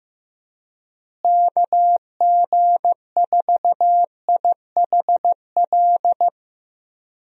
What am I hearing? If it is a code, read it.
Morse code, heard as a keyed tone, about 15 wpm, 700 Hz. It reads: KG4IHL
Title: USA Call Signs - 6 Characters - Repeated 2x - 15wpm